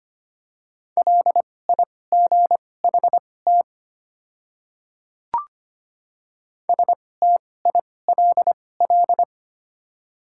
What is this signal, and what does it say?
Morse code, heard as a keyed tone, about 25 wpm, spaced as for 18 wpm, 690 Hz. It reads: LIGHT STILL
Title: 2x Character Spacing - Top 300 Words - 25wpm